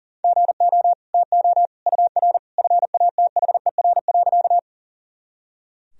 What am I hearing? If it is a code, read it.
Morse code, heard as a keyed tone, about 40 wpm, 700 Hz. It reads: GO TO UR FATHER.
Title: Sentences from Top 300 Words and Abbreviations 40wpm